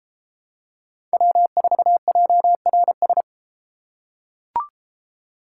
Morse code, heard as a keyed tone, about 33 wpm, 700 Hz. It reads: W4JRS